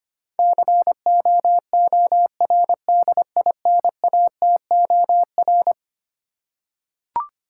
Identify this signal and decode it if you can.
Morse code, heard as a keyed tone, about 25 wpm, 700 Hz. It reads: COORDINATOR
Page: morse-code-ninja-practice-25wpm